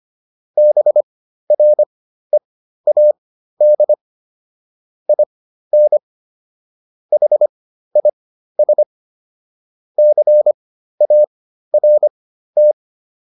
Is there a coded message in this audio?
Yes: Morse code from 0.6 to 12.7 s, heard as a keyed tone, about 25 wpm, spaced as for 13 wpm, 600 Hz. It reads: BREAD IN HIS CART